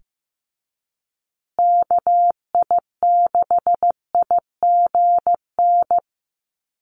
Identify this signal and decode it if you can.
Morse code, heard as a keyed tone, about 15 wpm, 700 Hz. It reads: KI6IGN
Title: USA Call Signs Repeated 2x 15wpm